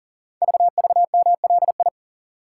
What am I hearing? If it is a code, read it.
Morse code, heard as a keyed tone, about 40 wpm, 700 Hz. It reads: VVMLI
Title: Code Groups - 5 Letters - 40wpm